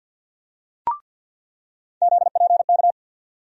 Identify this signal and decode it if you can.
Morse code, heard as a keyed tone, about 48 wpm, 700 Hz. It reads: ZPK
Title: USA Call Sign Postfixes - 3 Letters 50wpm